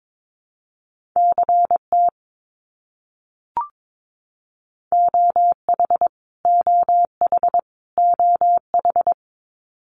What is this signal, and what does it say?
Morse code, heard as a keyed tone, about 22 wpm, 700 Hz. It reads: CT OH OH OH
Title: US States Repeated 3x 22wpm